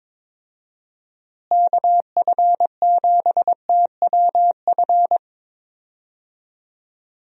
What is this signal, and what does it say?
Morse code, heard as a keyed tone, about 22 wpm, 700 Hz. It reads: KF7TWF
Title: USA Call Signs - 6 Characters - Repeated 3x - 22wpm